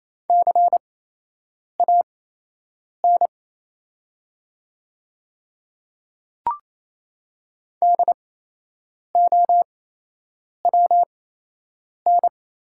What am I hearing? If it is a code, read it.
Morse code, heard as a keyed tone, about 28 wpm, spaced as for 8 wpm, 700 Hz. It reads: CAN DOWN
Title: 8x Character Spacing - Top 200 Words - 28wpm